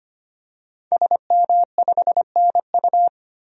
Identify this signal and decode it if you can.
Morse code, heard as a keyed tone, about 25 wpm, 700 Hz. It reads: SM5NU
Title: DXCC-100 Call Signs - 5 Characters - 25wpm